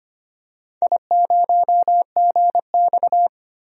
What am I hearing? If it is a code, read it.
Morse code, heard as a keyed tone, about 25 wpm, 700 Hz. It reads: I0GX
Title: DXCC-100 Call Signs - 4 Characters - 25wpm